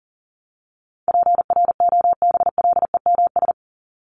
Morse code, heard as a keyed tone, about 40 wpm, 700 Hz. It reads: PROBLEMS
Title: ES:S Suffix - Top 100 Words with Base Words 40wpm